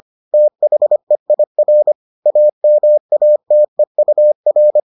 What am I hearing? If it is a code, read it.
Morse code, heard as a keyed tone, about 25 wpm, 600 Hz. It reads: THEIR AMATEUR